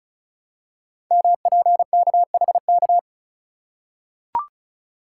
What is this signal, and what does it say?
Morse code, heard as a keyed tone, about 36 wpm, 690 Hz. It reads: MPKHK
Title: Code Groups - 5 Letter:Numbers - 35wpm